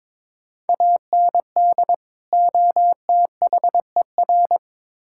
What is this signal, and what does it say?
Morse code, heard as a keyed tone, about 23 wpm, 700 Hz. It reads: AND OTHER
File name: Build A Phrase - 2-Word Phrases - 22wpm